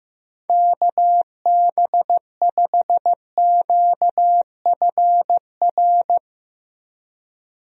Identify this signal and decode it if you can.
Morse code, heard as a keyed tone, about 15 wpm, 700 Hz. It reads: KB5QFR
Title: USA Call Signs - 6 Characters - Repeated 3x - 15wpm